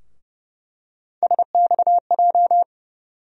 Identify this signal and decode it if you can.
Morse code, heard as a keyed tone, about 30 wpm, 700 Hz. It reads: SXJ
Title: USA Call Sign Postfixes - 3 Letters 30wpm